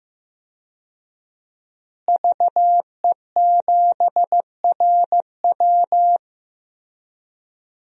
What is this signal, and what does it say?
Morse code, heard as a keyed tone, about 15 wpm, 700 Hz. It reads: VE7RW
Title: DXCC-100 Call Signs - 5 Characters - Repeated 3x - 15wpm